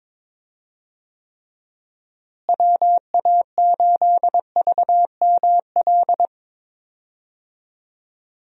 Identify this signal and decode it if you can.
Morse code, heard as a keyed tone, about 21 wpm, 700 Hz. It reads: WA8VML
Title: USA Call Signs - 6 Characters - Repeated 3x - 22wpm